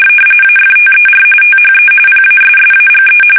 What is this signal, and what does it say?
RTTY, about 45.45 baud. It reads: AFTER THE PASSAGE
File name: fsk850 / usnavy1